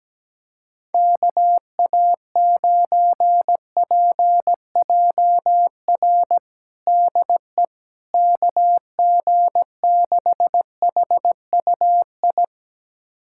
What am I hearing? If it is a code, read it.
Morse code, heard as a keyed tone, about 17 wpm, 700 Hz. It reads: KA9PJR DE KG6HUI
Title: USA Callsign DE Callsign 17wpm